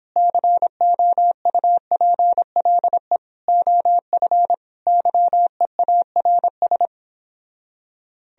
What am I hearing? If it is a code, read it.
Morse code, heard as a keyed tone, about 26 wpm, 700 Hz. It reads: COUPLE OF YEARS
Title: Speed-Racing - Sets of 4 Words 22wpm